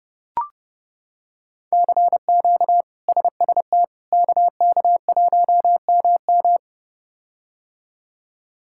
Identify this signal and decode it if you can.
Morse code, heard as a keyed tone, about 32 wpm, 700 Hz. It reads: CQ SST KK1MM
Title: K1USN SST QSO Practice - 30wpm